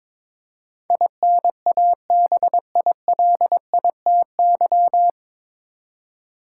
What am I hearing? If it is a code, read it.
Morse code, heard as a keyed tone, about 22 wpm, 700 Hz. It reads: INABILITY
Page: Morse Code Ninja practice 22wpm